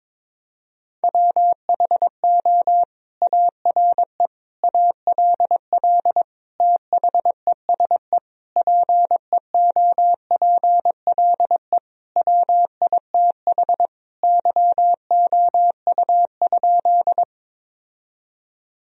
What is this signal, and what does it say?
Morse code, heard as a keyed tone, about 21 wpm, 700 Hz. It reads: WHO ARE ALL THESE PEOPLE WITH YOU?
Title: Sentences from Top 1000 Words - Review Part 1 - 22wpm